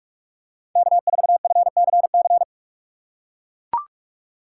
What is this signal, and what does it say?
Morse code, heard as a keyed tone, about 42 wpm, 700 Hz. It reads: K4FCC
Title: K1USN SST QSO Practice - 45wpm